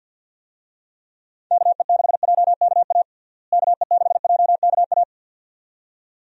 Morse code, heard as a keyed tone, about 52 wpm, 700 Hz. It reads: KE6JKA KE6JKA
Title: USA Call Signs - 6 Characters - Repeated 2x - 50wpm